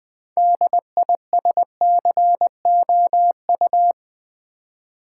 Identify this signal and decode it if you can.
Morse code, heard as a keyed tone, about 20 wpm, 700 Hz. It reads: DISCOU
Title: Build A Word - 3 Letters At A Time - 9-Letter Words - 20wpm